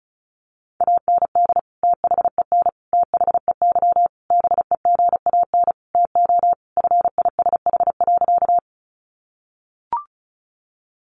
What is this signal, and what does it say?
Morse code, heard as a keyed tone, about 34 wpm, 700 Hz. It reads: AND THEN THEY BEGAN TO FISH.
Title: Sentences from Top 300 Words 35wpm